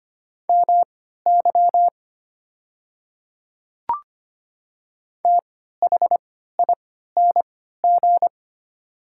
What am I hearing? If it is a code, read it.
Morse code, heard as a keyed tone, about 25 wpm, spaced as for 14 wpm, 700 Hz. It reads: MY THING
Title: 3x Character Spacing - Top 100 Words - 25wpm